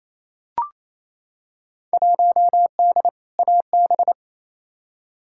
Morse code, heard as a keyed tone, about 28 wpm, 700 Hz. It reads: 1D AB
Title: Field Day Exchanges Encore - 28wpm